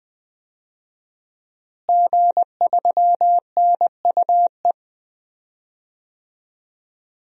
Morse code, heard as a keyed tone, about 20 wpm, 700 Hz. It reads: G3NUE
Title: DXCC-100 Call Signs - 5 Characters - Repeated 3x - 20wpm